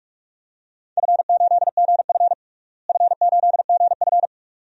Morse code, heard as a keyed tone, about 41 wpm, 700 Hz. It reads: F8GF F8GF
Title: DXCC-100 Call Signs - 4 Characters - Repeated 2x - 45wpm